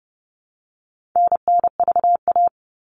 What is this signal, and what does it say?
Morse code, heard as a keyed tone, about 30 wpm, 700 Hz. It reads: NNVA
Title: All Cut Numbers - 4-Digit 30wpm